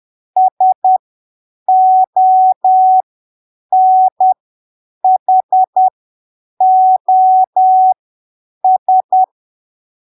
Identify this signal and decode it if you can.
Morse code, heard as a keyed tone, about 10 wpm, spaced as for 7 wpm, 750 Hz. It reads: SONHOS
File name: arquivos / texto / texto 15 10ppm